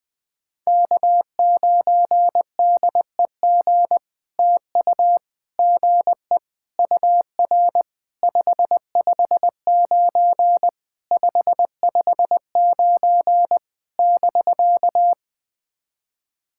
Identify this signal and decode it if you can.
Morse code, heard as a keyed tone, about 20 wpm, 700 Hz. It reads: K9DEG TU GE UR 559 559 <BK>
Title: POTA QSO Practice - 20wpm